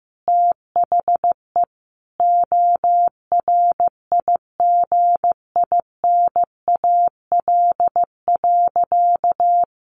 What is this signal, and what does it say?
Morse code, heard as a keyed tone, about 15 wpm, 700 Hz. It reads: THE ORIGINAL.
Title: Sentences from Top 1400 Words 15wpm